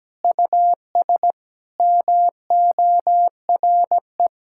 Morse code, heard as a keyed tone, about 17 wpm, 700 Hz. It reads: US MORE
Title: Sentences from Top 1600 Words 17wpm